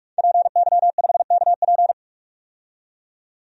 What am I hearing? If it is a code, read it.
Morse code, heard as a keyed tone, about 46 wpm, 700 Hz. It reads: PY5KP
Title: DXCC-100 Call Signs - 5 Characters - 45wpm